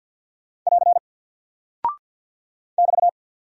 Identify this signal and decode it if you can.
Morse code, heard as a keyed tone, about 49 wpm, 700 Hz. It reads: <AR> <BT>
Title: Single Character (Letter:Number:Punctuation:Prosign) Alt 50wpm